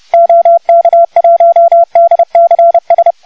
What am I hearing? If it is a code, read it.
Morse code, heard as a keyed tone, about 30 wpm, 1340 Hz, with noise. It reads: OK1DCS
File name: Archiv / CUC435 / OK1DCS CUC435